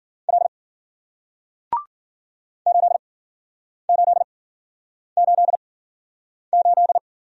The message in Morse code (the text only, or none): L 8888